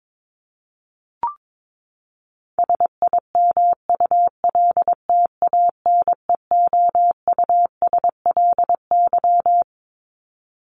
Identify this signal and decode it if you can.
Morse code, heard as a keyed tone, about 22 wpm, 700 Hz. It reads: SIMULTANEOUSLY
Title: LY Suffix - Top 100 Words with Base Words 22wpm